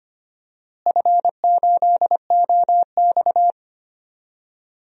Morse code, heard as a keyed tone, about 25 wpm, 700 Hz. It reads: F8OX
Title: DXCC-100 Call Signs - 4 Characters - Repeated 2x - 25wpm